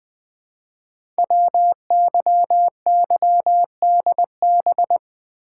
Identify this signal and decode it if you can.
Morse code, heard as a keyed tone, about 20 wpm, 700 Hz. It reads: WYYDB
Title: Code Groups - 5 Letters - 20wpm